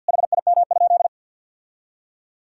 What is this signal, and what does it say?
Morse code, heard as a keyed tone, about 52 wpm, 700 Hz. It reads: HIM?
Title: Sentences from Top 1000 Words - Review Part 2 - 50wpm